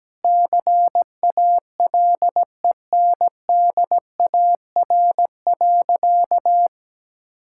Morse code, heard as a keyed tone, about 17 wpm, 700 Hz. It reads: CALENDAR.
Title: Sentences from Top 1600 Words 17wpm